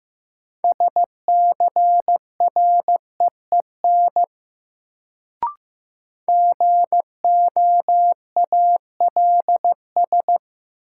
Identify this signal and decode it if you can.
Morse code, heard as a keyed tone, about 15 wpm, 700 Hz. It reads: SCREEN GOALS
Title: Sentences from Top 1000 Words - Review Part 6 - New Word Drill - 15wpm